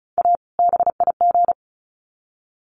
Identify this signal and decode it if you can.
Morse code, heard as a keyed tone, about 34 wpm, 700 Hz. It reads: A BIG
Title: Sets of 2 Words 35wpm